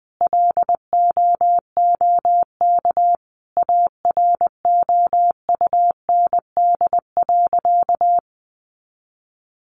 Morse code, heard as a keyed tone, about 20 wpm, 700 Hz. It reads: LOOK AROUND.
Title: Sentences from Top 600 Words 20wpm